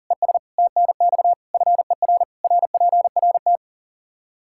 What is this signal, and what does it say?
Morse code, heard as a keyed tone, about 37 wpm, 700 Hz. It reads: ES TNX FER RPRT
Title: Practice QSO - Exchange 2 - Info - 40wpm